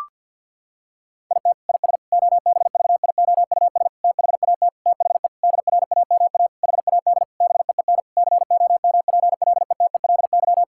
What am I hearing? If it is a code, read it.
Morse code, heard as a keyed tone, about 50 wpm, 700 Hz. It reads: IT IS OBVIOUS THAT THE DRAMA HAD BEEN COMPLETELY
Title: Sentences from Top 1600 Words 50wpm